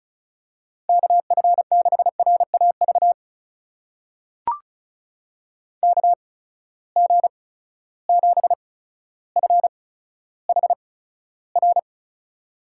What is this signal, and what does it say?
Morse code, heard as a keyed tone, about 34 wpm, spaced as for 15 wpm, 700 Hz. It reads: KF6RAV KG7FHR
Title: USA Call Signs - 6 Characters - 8x Character Spacing - 35wpm